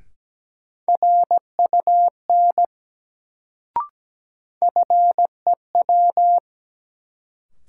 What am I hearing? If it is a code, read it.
Morse code, heard as a keyed tone, about 17 wpm, 700 Hz. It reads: RUN FEW
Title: Sentences from Top 300 Words - New Word Drill - 17wpm